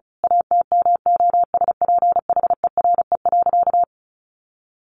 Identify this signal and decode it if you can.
Morse code, heard as a keyed tone, about 34 wpm, 700 Hz. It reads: ATMOSPHERE.
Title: A Sound Of Thunder 35wpm